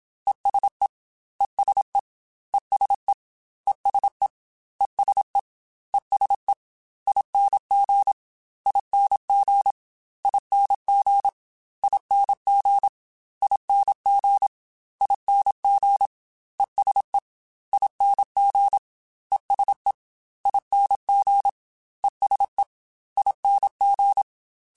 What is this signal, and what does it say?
Morse code, heard as a keyed tone, about 27 wpm, 800 Hz. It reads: ESE ESE ESE ESE ESE ESE ING ING ING ING ING ING ESE ING ESE ING ESE ING